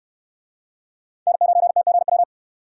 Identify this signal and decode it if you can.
Morse code, heard as a keyed tone, about 68 wpm, 700 Hz. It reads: N9TGW